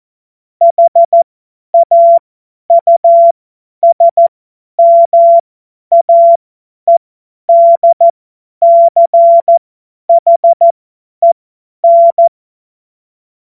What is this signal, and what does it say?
Morse code, heard as a keyed tone, about 14 wpm, spaced as for 10 wpm, 680 Hz. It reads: HAUSMAEDCHEN
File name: Morsekurs / Witze / Witz 22. Februar 2022 50 F70